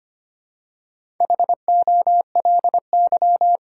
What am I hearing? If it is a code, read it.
Morse code, heard as a keyed tone, about 25 wpm, 700 Hz. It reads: HOLY
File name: practice / 25wpm / All-Age Appropriate Jokes - 25wpm